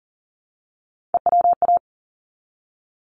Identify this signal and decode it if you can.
Morse code, heard as a keyed tone, about 39 wpm, 700 Hz. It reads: EWA
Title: ARRL Sections 40wpm